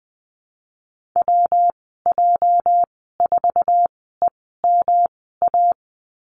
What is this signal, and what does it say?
Morse code, heard as a keyed tone, about 20 wpm, spaced as for 14 wpm, 700 Hz. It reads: WJ4EMA
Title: 207 Lesson 33g - Letter j - Call Signs 20wpm